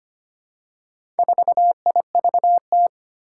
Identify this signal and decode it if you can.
Morse code, heard as a keyed tone, about 25 wpm, 700 Hz. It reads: 4IVT
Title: Four Challenging Letter-Number Combinations 25wpm